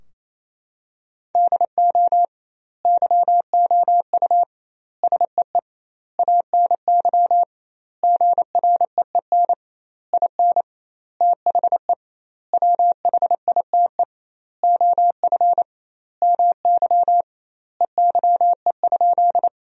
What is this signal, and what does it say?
Morse code, heard as a keyed tone, about 28 wpm, 700 Hz. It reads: DO YOU SEE ANY GREEN IN THE WHITE OF MY EYE?
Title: 2x Word-Spacing - Sentences from Top 500 Words - 28wpm